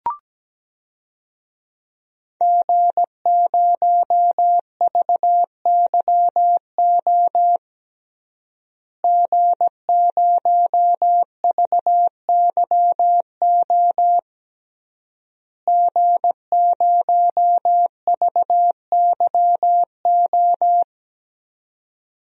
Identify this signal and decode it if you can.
Morse code, heard as a keyed tone, about 17 wpm, 700 Hz. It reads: G0VYO G0VYO G0VYO
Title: DXCC-100 Call Signs - 5 Characters - Repeated 3x - 17wpm